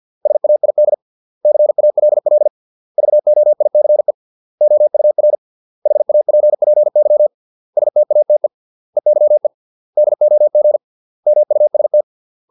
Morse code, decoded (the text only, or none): SAID CALL VOICE OUR HAPPY STATE EYE DOG MUST